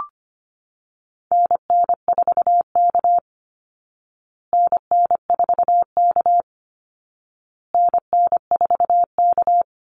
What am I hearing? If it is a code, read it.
Morse code, heard as a keyed tone, about 25 wpm, 700 Hz. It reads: NN4K NN4K NN4K
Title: Call Signs Repeated 2x - 25wpm